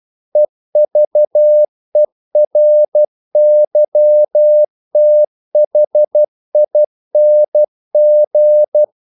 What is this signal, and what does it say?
Morse code, heard as a keyed tone, about 12 wpm, 600 Hz. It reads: EVERYTHING
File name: CW / Practice / EnglishQuotations0112